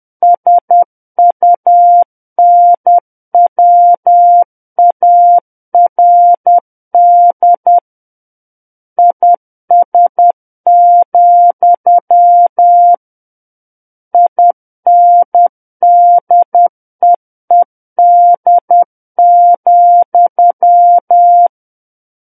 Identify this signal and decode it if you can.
Morse code, heard as a keyed tone, about 10 wpm, 700 Hz. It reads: SUNWARD IS, INDEED,
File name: Book One / 01-Chapter 1 - 10 WPM